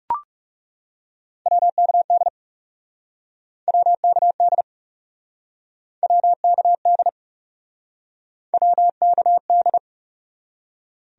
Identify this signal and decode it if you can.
Morse code, heard as a keyed tone, about 37 wpm, 700 Hz. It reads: WKD WKD WKD WKD
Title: Speed-Racing Abbreviations - 30wpm